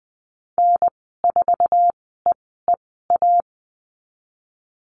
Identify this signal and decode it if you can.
Morse code, heard as a keyed tone, about 20 wpm, spaced as for 14 wpm, 700 Hz. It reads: N4EEA
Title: 062 Lesson 11g - Letter h - Call Signs 20wpm